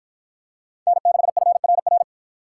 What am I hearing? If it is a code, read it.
Morse code, heard as a keyed tone, about 48 wpm, 700 Hz. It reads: N6FRR